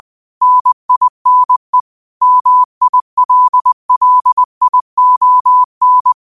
Morse code, heard as a keyed tone, about 20 wpm, 1000 Hz. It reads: NINE MILLION
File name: qotdfiles / 20wpm